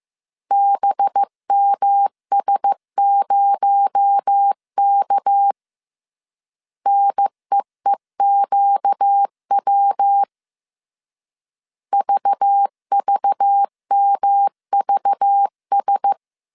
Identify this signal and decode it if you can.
Morse code, heard as a keyed tone, about 15 wpm, 800 Hz. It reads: BMS0K NEEQW VVMVS